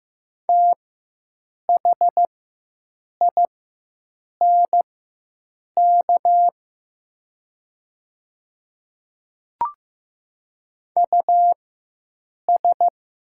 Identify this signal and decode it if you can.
Morse code, heard as a keyed tone, about 15 wpm, spaced as for 7 wpm, 700 Hz. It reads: THINK US
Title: 4x Character Spacing - Top 200 Words - 15wpm